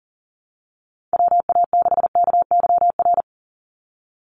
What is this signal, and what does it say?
Morse code, heard as a keyed tone, about 39 wpm, 700 Hz. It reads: WA6KYR
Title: NA SOTA Chaser Callsigns 40wpm